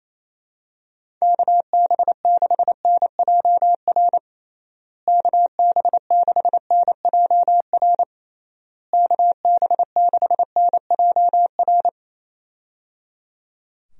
Morse code, heard as a keyed tone, about 27 wpm, 700 Hz. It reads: KB6NJR KB6NJR KB6NJR
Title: USA Call Signs - 6 Characters - Repeated 3x - 28wpm